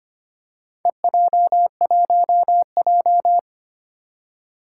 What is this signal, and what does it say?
Morse code, heard as a keyed tone, about 25 wpm, 700 Hz. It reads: EJ1J
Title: Four Challenging Letter-Number Combinations 25wpm